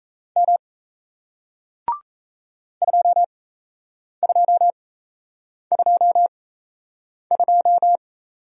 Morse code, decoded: M 2222

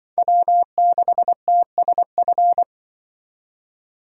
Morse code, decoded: W6TSF

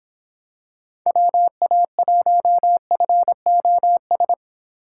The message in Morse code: WA1FOS

